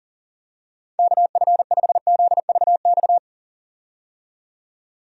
Morse code, heard as a keyed tone, about 40 wpm, 700 Hz. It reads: KF5ZVX
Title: USA Call Signs - 6 Characters - 40wpm